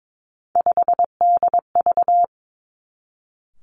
Morse code, heard as a keyed tone, about 22 wpm, 700 Hz. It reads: 5DV
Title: Three Challenging Letter-Number Combinations 22wpm